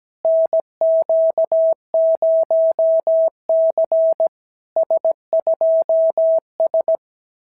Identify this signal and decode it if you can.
Morse code, heard as a keyed tone, about 17 wpm, 650 Hz. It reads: NQ0C S2S
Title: SOTA QSO Practice 17wpm